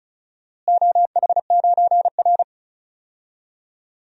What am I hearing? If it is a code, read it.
Morse code, heard as a keyed tone, about 36 wpm, 700 Hz. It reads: OH9R
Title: DXCC-100 Call Signs - 4 Characters - Repeated 3x - 35wpm